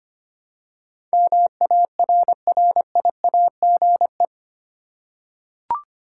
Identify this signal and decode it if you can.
Morse code, heard as a keyed tone, about 25 wpm, 700 Hz. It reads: MARRIAGE
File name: Eight Letter Words - 25wpm